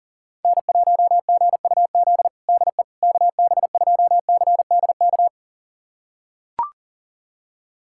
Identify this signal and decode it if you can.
Morse code, heard as a keyed tone, about 40 wpm, 700 Hz. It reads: N1GUZ DE KB2CDK
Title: USA Callsign DE Callsign 40wpm